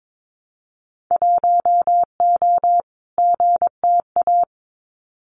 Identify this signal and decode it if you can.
Morse code, heard as a keyed tone, about 22 wpm, 700 Hz. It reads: 1O GTA
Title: Winter Field Day Exchanges 22wpm